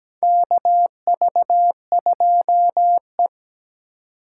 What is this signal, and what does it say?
Morse code, heard as a keyed tone, about 17 wpm, 700 Hz. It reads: KV2E
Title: USA Call Signs - 3x Character Spacing - 17wpm